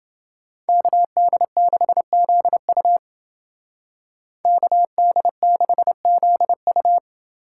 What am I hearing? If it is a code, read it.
Morse code, heard as a keyed tone, about 28 wpm, 700 Hz. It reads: KD6ZU KD6ZU